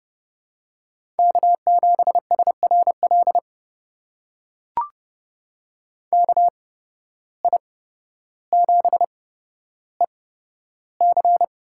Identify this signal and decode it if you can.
Morse code, heard as a keyed tone, about 29 wpm, spaced as for 13 wpm, 700 Hz. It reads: K7SRL KI7EC
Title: USA Call Signs - 5 Characters - 8x Character Spacing - 30wpm